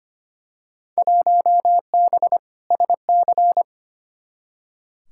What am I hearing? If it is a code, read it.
Morse code, heard as a keyed tone, about 25 wpm, 700 Hz. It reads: 1B SC
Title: Field Day Exchanges - 25wpm